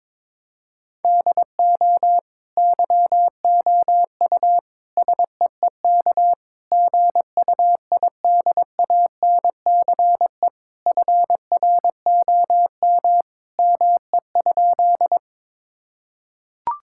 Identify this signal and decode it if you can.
Morse code, heard as a keyed tone, about 22 wpm, 700 Hz. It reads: DO YOU SEEK GUIDANCE FROM ME?
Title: Sentences from Top 1600 Words 22wpm